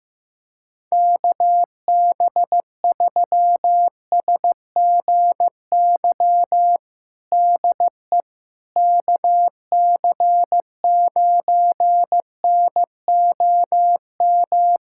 Morse code, heard as a keyed tone, about 16 wpm, 700 Hz. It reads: KB3SGY DE KC9NOM
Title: USA Callsign DE Callsign - 6 Characters - 15wpm